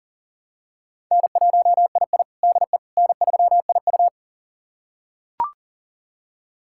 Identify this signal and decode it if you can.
Morse code, heard as a keyed tone, about 40 wpm, 700 Hz. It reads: N1II DE N3IU
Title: USA Callsign DE Callsign - 4 Characters - 40wpm